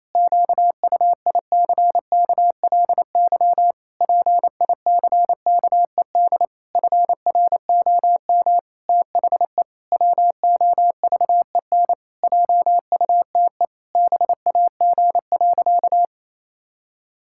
Morse code, heard as a keyed tone, about 28 wpm, 700 Hz. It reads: QUICKLY PICKED FROM THE WOVEN JUTE BAG.
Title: Pangram Challenge v2 - 28wpm